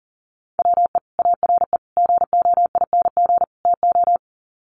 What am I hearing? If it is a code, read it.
Morse code, heard as a keyed tone, about 40 wpm, 700 Hz. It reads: WE ARE GOING TO
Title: Sets of 4 Words 40wpm